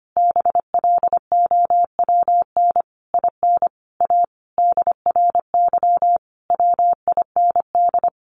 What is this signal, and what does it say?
Morse code, heard as a keyed tone, about 25 wpm, 700 Hz. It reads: BLOWN IN A DRY WIND
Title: A Sound Of Thunder 25wpm